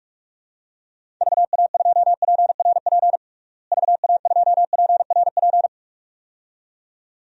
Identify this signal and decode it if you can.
Morse code, heard as a keyed tone, about 46 wpm, 700 Hz. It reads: VA2PRP VA2PRP
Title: DXCC-100 Call Signs - 6 Characters - Repeated 2x - 45wpm